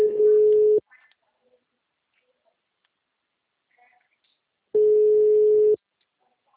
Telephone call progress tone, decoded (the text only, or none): ringback tone